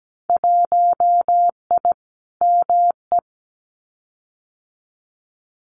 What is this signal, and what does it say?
Morse code, heard as a keyed tone, about 17 wpm, 700 Hz. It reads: 1I ME
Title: Winter Field Day Exchanges 17wpm